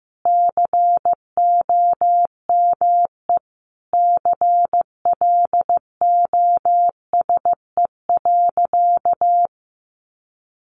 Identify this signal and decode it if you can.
Morse code, heard as a keyed tone, about 14 wpm, 700 Hz. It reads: COME CLOSE.
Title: Sentences from Top 300 Words 15wpm